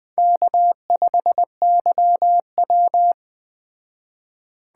Morse code, heard as a keyed tone, about 20 wpm, 700 Hz. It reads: K5YW